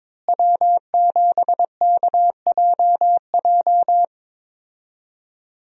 Morse code, heard as a keyed tone, about 22 wpm, 700 Hz. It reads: W7KJJ
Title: USA Call Signs - 5 Characters - Repeated 2x - 22wpm